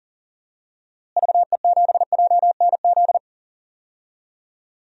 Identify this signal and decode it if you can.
Morse code, heard as a keyed tone, about 40 wpm, 700 Hz. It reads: VE7JNZ
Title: DXCC-100 Call Signs - 6 Characters - Repeated 2x - 40wpm